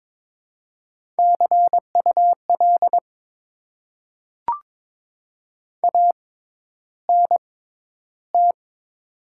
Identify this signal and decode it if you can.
Morse code, heard as a keyed tone, about 22 wpm, spaced as for 11 wpm, 700 Hz. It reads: CUL ANT
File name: Top 90 QSO Elements - 6x Character Spacing - 22wpm